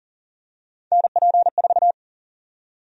Morse code, heard as a keyed tone, about 40 wpm, 700 Hz. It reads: NP4
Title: USA Call Sign Prefixes - 40wpm